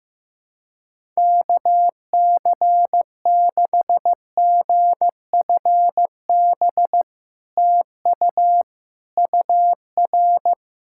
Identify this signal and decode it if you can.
Morse code, heard as a keyed tone, about 15 wpm, 700 Hz. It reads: KC6GFB TU UR
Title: SOTA QSO Practice 15wpm